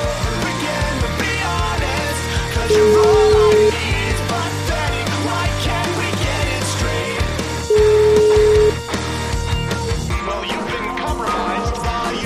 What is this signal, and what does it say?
Ringback tone (Europe, 425 Hz), with noise.